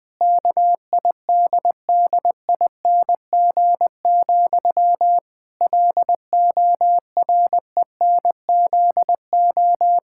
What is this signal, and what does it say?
Morse code, heard as a keyed tone, about 20 wpm, 700 Hz. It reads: KIDDING, LORENZO
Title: Pangram Challenge v2 - 20wpm